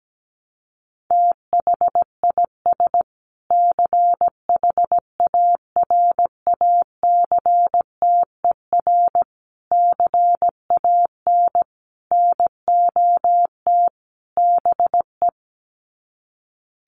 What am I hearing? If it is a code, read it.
Morse code, heard as a keyed tone, about 17 wpm, 700 Hz. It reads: THIS CHARACTER CAN NOT BE
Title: Sets of 5 Words Encore2 17wpm